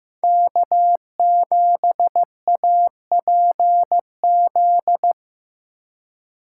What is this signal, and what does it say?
Morse code, heard as a keyed tone, about 18 wpm, 700 Hz. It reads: K7APZ
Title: Speed Racing - USA 1x3 Call Signs 15wpm